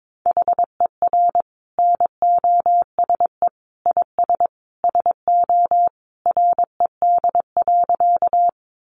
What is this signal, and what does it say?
Morse code, heard as a keyed tone, about 22 wpm, 700 Hz. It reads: HER NOSE IS SO RED.
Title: Easy Phrases 22wpm